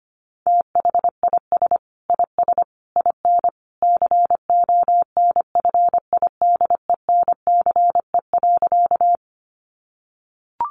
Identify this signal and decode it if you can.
Morse code, heard as a keyed tone, about 25 wpm, 700 Hz. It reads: THIS IS IN CONFIDENCE.
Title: Sentences from Top 1500 Words 25wpm